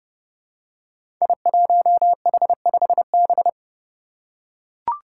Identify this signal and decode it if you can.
Morse code, heard as a keyed tone, about 30 wpm, 700 Hz. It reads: I1H5B
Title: Five Challenging Letter-Number Combinations 30wpm